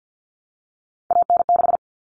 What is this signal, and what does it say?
Morse code, heard as a keyed tone, about 50 wpm, 700 Hz. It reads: AN6
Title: Common Cut Numbers - 3-Digit 50wpm